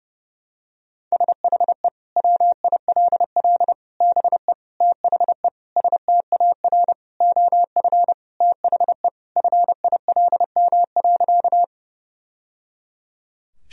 Morse code, heard as a keyed tone, about 30 wpm, 700 Hz. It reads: SHE WILL BE THE STAR OF THE FILM.
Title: Sentences from Top 1000 Words - Review Part 3 - 30wpm